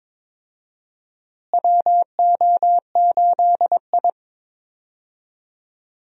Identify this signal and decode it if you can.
Morse code, heard as a keyed tone, about 22 wpm, 700 Hz. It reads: WO8I
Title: USA Call Signs - 4 Characters - Repeated 3x - 22wpm